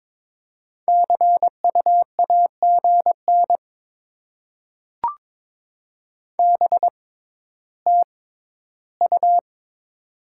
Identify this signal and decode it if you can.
Morse code, heard as a keyed tone, about 22 wpm, spaced as for 13 wpm, 700 Hz. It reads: CUAGN BTU
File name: Top 20 QSO Elements - 6x Character Spacing - 22wpm